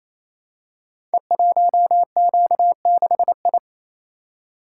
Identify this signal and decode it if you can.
Morse code, heard as a keyed tone, about 29 wpm, 700 Hz. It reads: E1Q6I
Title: Code Groups - 5 Letter:Numbers - 28wpm